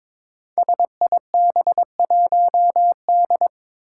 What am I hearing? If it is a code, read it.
Morse code, heard as a keyed tone, about 21 wpm, 700 Hz. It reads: SIB1D
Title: Five Challenging Letter-Number Combinations 22wpm